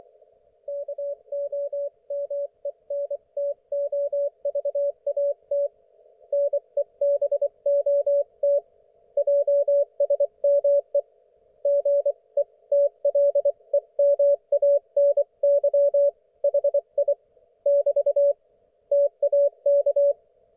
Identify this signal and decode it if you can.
Morse code, heard as a keyed tone, about 24 wpm, 570 Hz. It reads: KOMENTOVAT NEBOT JSME GETLEMANY HI <BT> TAK
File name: Archiv / CUC224 / ok1fhi cul224